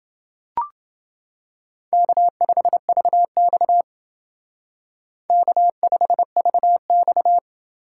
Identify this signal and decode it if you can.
Morse code, heard as a keyed tone, about 29 wpm, 700 Hz. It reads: K5VX K5VX